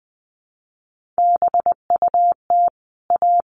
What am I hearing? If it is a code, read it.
Morse code, heard as a keyed tone, about 20 wpm, 700 Hz. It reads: BUT A
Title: Sentences from Top 1400 Words 20wpm